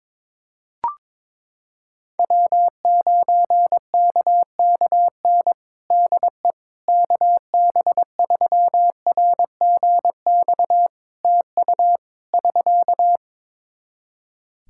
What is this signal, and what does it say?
Morse code, heard as a keyed tone, about 22 wpm, 700 Hz. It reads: W9KKN DE KB3RGX TU <SK>
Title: Practice QSO - Exchange 4 - Ending - 22wpm